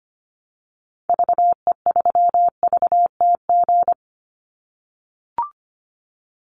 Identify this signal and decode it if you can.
Morse code, heard as a keyed tone, about 25 wpm, 700 Hz. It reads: VE3VTG
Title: Call Signs - 25wpm